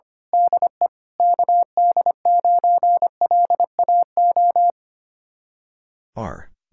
Morse code, heard as a keyed tone, about 25 wpm, 700 Hz. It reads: DE KD9LAO
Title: Speed-Racing Common QSO Phrases - 25wpm